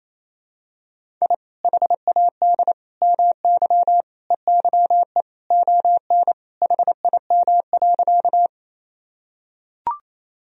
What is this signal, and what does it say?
Morse code, heard as a keyed tone, about 28 wpm, 700 Hz. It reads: I HAD MY EYE ON HIM.
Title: Sentences from Top 1000 Words - Review Part 1 - 28wpm